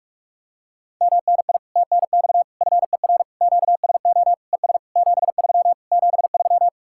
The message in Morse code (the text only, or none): MNI TNX FER QSO ES 73 73